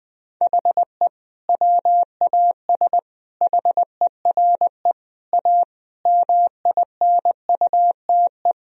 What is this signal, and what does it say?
Morse code, heard as a keyed tone, about 19 wpm, 700 Hz. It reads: HE WAS HERE A MINUTE
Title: Sentences from Top 1000 Words - Review Part 3 - 20wpm